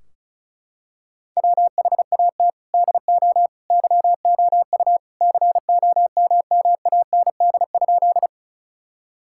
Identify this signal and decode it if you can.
Morse code, heard as a keyed tone, about 34 wpm, 700 Hz. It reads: WHAT DO YOU COMMAND?
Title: Sentences from Top 1600 Words 35wpm